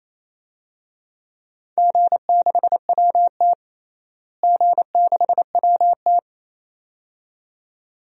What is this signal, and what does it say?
Morse code, heard as a keyed tone, about 27 wpm, 700 Hz. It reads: G6WT G6WT
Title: DXCC-100 Call Signs - 4 Characters - Repeated 2x - 28wpm